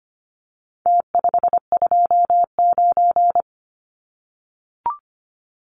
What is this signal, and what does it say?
Morse code, heard as a keyed tone, about 25 wpm, 700 Hz. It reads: T529
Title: Numbers - Contest Style - 4-Digit 25wpm